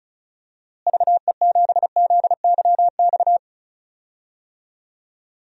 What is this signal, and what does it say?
Morse code, heard as a keyed tone, about 36 wpm, 700 Hz. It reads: VE7ZYX
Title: DXCC-100 Call Signs - 6 Characters - Repeated 2x - 35wpm